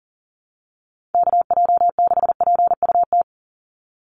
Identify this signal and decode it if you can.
Morse code, heard as a keyed tone, about 40 wpm, 700 Hz. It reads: KJ6PUT
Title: USA Call Signs 40wpm